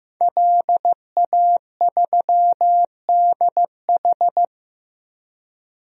Morse code, heard as a keyed tone, about 15 wpm, 700 Hz. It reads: LA3DH
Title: DXCC-100 Call Signs - 5 Characters - Repeated 2x - 15wpm